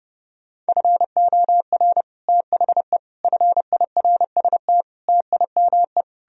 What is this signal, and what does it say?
Morse code, heard as a keyed tone, about 38 wpm, 700 Hz. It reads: FOR THE FIRST TIME